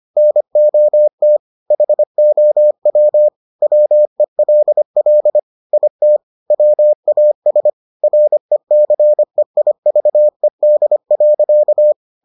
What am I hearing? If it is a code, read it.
Morse code, heard as a keyed tone, about 25 wpm, 600 Hz. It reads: NOT HOW WELL IT WAS RECEIVED.